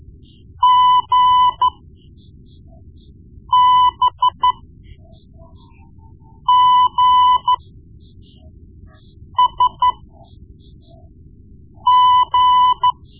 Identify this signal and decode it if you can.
Morse code, heard as a keyed tone, about 13 wpm, spaced as for 4 wpm, 1000 Hz. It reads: GBGSG
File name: Morse Lessons / Morse Lessons / Tape-1B